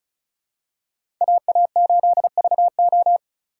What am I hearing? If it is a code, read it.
Morse code, heard as a keyed tone, about 34 wpm, 700 Hz. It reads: AA8VO
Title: USA Call Signs - 5 Characters - 2x Character Spacing - 35wpm